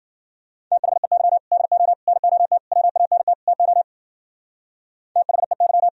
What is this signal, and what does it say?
Morse code, heard as a keyed tone, about 50 wpm, 700 Hz. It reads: THEY DO NOT WANT TO THEY
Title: Speed-Racing - Sets of 5 Words 40wpm